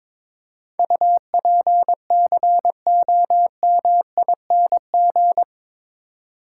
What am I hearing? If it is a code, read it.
Morse code, heard as a keyed tone, about 22 wpm, 700 Hz. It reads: UPCOMING
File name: practice / 22wpm / Eight Letter Words - 22wpm